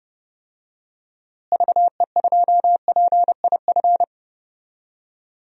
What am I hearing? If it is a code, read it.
Morse code, heard as a keyed tone, about 30 wpm, 700 Hz. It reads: VE2PIF
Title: DXCC-100 Call Signs - 6 Characters - Repeated 2x - 30wpm